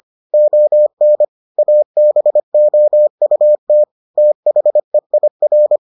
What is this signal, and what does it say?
Morse code, heard as a keyed tone, about 25 wpm, 600 Hz. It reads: ON ABOUT THEIR